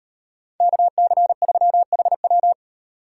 Morse code, heard as a keyed tone, about 38 wpm, 700 Hz. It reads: KC3HW